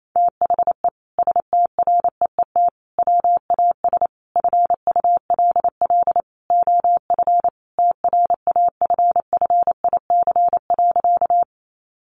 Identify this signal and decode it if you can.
Morse code, heard as a keyed tone, about 28 wpm, 700 Hz. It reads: THE STREET WAS FULL OF TRAFFIC.
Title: Sentences from Top 1300 Words 28wpm